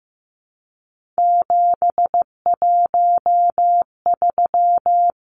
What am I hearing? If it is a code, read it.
Morse code, heard as a keyed tone, about 15 wpm, 700 Hz. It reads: 713
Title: Numbers - 3-Digit 15wpm